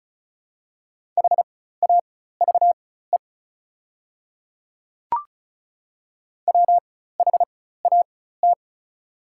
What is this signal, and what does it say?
Morse code, heard as a keyed tone, about 36 wpm, spaced as for 16 wpm, 700 Hz. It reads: HAVE WHAT
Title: 4x Character Spacing - Top 100 Words - 35wpm